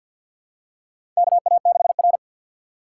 Morse code, heard as a keyed tone, about 53 wpm, 700 Hz. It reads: KA6R
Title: USA Call Signs - 3x Character Spacing - 50wpm